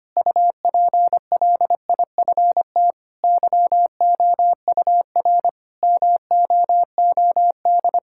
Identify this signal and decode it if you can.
Morse code, heard as a keyed tone, about 25 wpm, 700 Hz. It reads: UPLIFT YOUR MOOD
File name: practice / 25wpm / Difficult Sentences with Hints - 25wpm